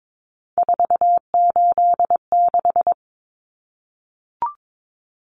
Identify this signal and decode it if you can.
Morse code, heard as a keyed tone, about 22 wpm, 700 Hz. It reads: 486
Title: Common Cut Numbers - 3-Digit 22wpm